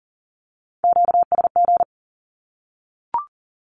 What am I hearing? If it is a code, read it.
Morse code, heard as a keyed tone, about 40 wpm, 700 Hz. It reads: QSG
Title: Q-Codes Repeated 3x 40wpm